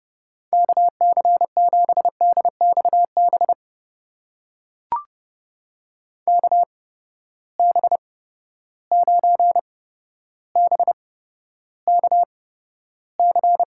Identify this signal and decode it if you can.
Morse code, heard as a keyed tone, about 30 wpm, spaced as for 13 wpm, 700 Hz. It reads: KC7DXB KB9BKC
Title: USA Call Signs - 6 Characters - 8x Character Spacing - 30wpm